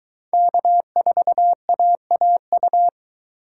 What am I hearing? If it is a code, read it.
Morse code, heard as a keyed tone, about 23 wpm, 700 Hz. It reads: K4AAU